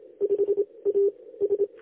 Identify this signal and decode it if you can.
Morse code, heard as a keyed tone, about 27 wpm, 400 Hz. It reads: 5 A S
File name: lx1nj- -1005utc-10110khz2-49235